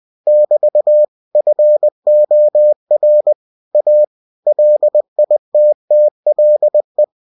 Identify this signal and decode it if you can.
Morse code, heard as a keyed tone, about 20 wpm, 600 Hz. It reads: <BT> FOR A LITTLE
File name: cw-books / winnie-the-pooh / winnie-20wpm-page-0095